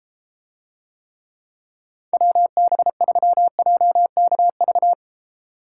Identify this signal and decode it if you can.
Morse code, heard as a keyed tone, about 32 wpm, 700 Hz. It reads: WB3JKV